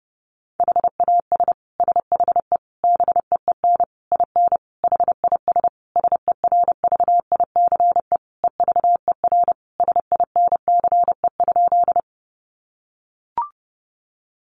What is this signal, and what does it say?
Morse code, heard as a keyed tone, about 30 wpm, 700 Hz. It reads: HAS SHE BEEN IN HIS SERVICE EVER SINCE?
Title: Sentences from Top 700 Words 30wpm